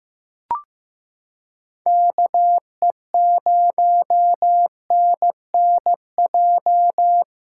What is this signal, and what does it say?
Morse code, heard as a keyed tone, about 15 wpm, 700 Hz. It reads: KE0NNJ
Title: USA Callsign DE Callsign - 6 Characters - 15wpm